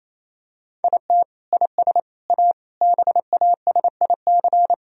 Morse code, heard as a keyed tone, about 28 wpm, 700 Hz. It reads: IT IS A BASIC
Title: Sentences from Top 1000 Words - Review Part 3 - 28wpm